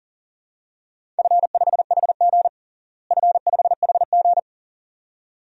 Morse code, heard as a keyed tone, about 40 wpm, 700 Hz. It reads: F5HG F5HG
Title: DXCC-100 Call Signs - 4 Characters - Repeated 2x - 40wpm